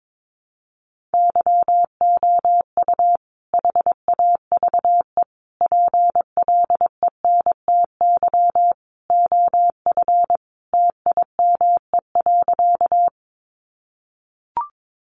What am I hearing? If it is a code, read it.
Morse code, heard as a keyed tone, about 22 wpm, 700 Hz. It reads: YOU HAVE PLENTY OF TIME.
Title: Sentences from Top 1300 Words 22wpm